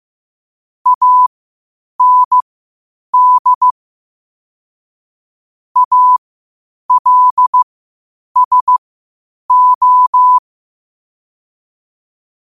Morse code, heard as a keyed tone, about 15 wpm, spaced as for 8 wpm, 1000 Hz. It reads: AND ALSO
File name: qotdfiles / 8wpm